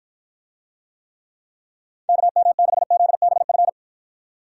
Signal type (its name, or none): Morse code